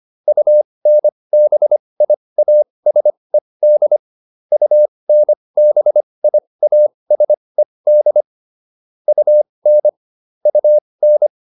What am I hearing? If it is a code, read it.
Morse code, heard as a keyed tone, about 25 wpm, spaced as for 20 wpm, 600 Hz. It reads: UNBIASED UNBIASED UN UN